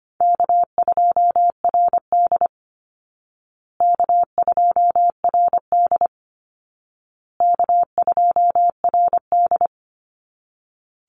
Morse code, heard as a keyed tone, about 25 wpm, 700 Hz. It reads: K2RD K2RD K2RD
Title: Easy Call Signs Repeated 3x - 25wpm